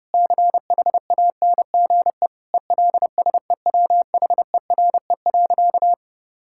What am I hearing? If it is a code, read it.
Morse code, heard as a keyed tone, about 30 wpm, 700 Hz. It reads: CHANGE ELSEWHERE.
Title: Sentences from Top 1600 Words 30wpm